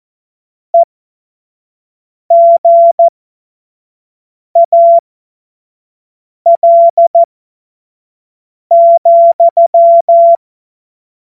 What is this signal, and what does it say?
Morse code, heard as a keyed tone, about 14 wpm, spaced as for 5 wpm, 680 Hz. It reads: EGAL,
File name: Morsekurs / Witze / Witz 30 Januar 2024 25 F70